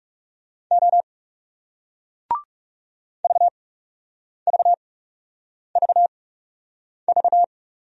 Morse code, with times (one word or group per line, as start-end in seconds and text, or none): 0.7-1.0 O
3.2-7.4 VVVV